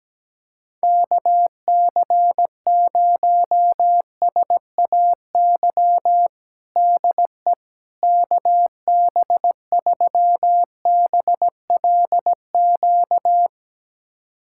Morse code, heard as a keyed tone, about 17 wpm, 700 Hz. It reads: KC0SAY DE KB3BLQ
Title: USA Callsign DE Callsign - 6 Characters - 17wpm